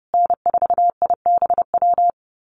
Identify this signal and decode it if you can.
Morse code, heard as a keyed tone, about 28 wpm, 700 Hz. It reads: N4IBW
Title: USA Call Signs Repeated 3x 30wpm